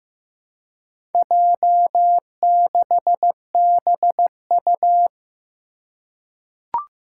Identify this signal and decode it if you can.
Morse code, heard as a keyed tone, about 15 wpm, 700 Hz. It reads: J6BU
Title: Four Challenging Letter-Number Combinations 15wpm